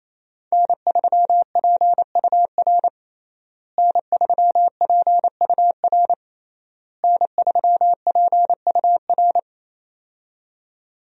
Morse code, heard as a keyed tone, about 28 wpm, 700 Hz. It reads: N3PUR N3PUR N3PUR
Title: USA Call Signs - 5 Characters - Repeated 3x - 28wpm